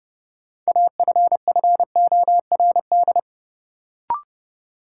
Morse code, heard as a keyed tone, about 30 wpm, 700 Hz. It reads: AFFORD
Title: Sentences from Top 1100 Words - New Word Drill - 30wpm